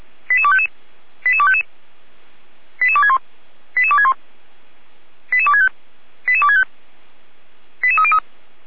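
Five-tone selective call (ZVEI), heard as quick runs of five tones, tones of about 80 ms. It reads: 88255 88255 88261 88261 88265 88265 88303
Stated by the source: Five-tone ZVEI selective call received by SDR.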